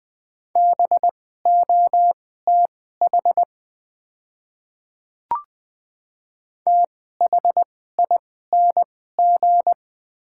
Morse code, heard as a keyed tone, about 20 wpm, spaced as for 14 wpm, 700 Hz. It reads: BOTH THING
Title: 2x Character Spacing - Top 500 Words - 20wpm